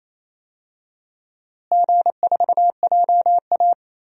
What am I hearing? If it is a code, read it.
Morse code, heard as a keyed tone, about 28 wpm, 700 Hz. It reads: G4JA
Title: DXCC-100 Call Signs - 4 Characters - Repeated 2x - 28wpm